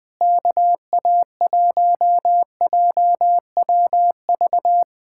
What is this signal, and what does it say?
Morse code, heard as a keyed tone, about 20 wpm, 700 Hz. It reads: KA1JWV
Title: Speed-Racing Common QSO Phrases - 17wpm